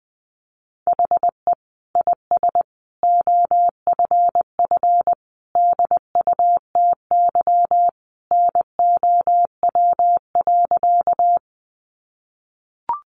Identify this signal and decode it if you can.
Morse code, heard as a keyed tone, about 20 wpm, 700 Hz. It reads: HE IS OFF DUTY NOW.
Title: Sentences from Top 1500 Words 20wpm